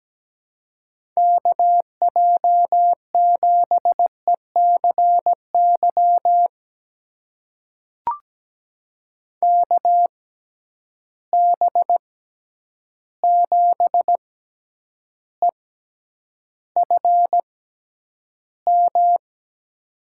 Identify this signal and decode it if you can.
Morse code, heard as a keyed tone, about 17 wpm, spaced as for 9 wpm, 700 Hz. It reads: KJ7ECY KB7EFM
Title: USA Call Signs - 6 Characters - 6x Character Spacing - 17wpm